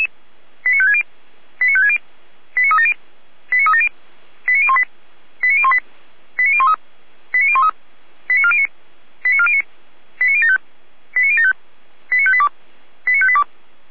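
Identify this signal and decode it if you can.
Five-tone selective call (ZVEI), heard as quick runs of five tones, tones of about 70 ms. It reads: 89560 89560 89370 89370 89018 89018 89013 89013 89409 89409 89075 89075 89582 89582
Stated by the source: Five-tone ZVEI selective call received by SDR, recorded by Feuerwehr Orlamünde.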